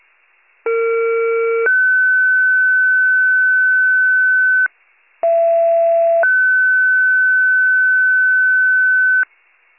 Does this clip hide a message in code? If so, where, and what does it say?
Yes: two-tone page from 0.7 to 9.2 s, heard as a one-second tone, then a three-second tone, about 460 then 1500 Hz, about 670 then 1500 Hz.